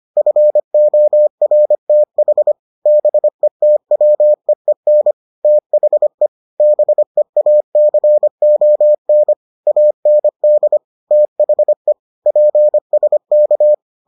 Morse code, heard as a keyed tone, about 25 wpm, 600 Hz. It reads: FORTH BETWEEN THE BEACON AND THE PSK